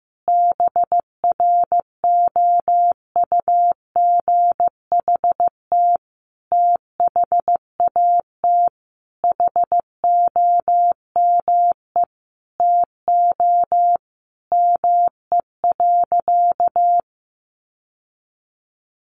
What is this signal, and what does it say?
Morse code, heard as a keyed tone, about 15 wpm, 700 Hz. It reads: BROUGHT THAT HOME TO ME.
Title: Sentences from Top 1100 Words 15wpm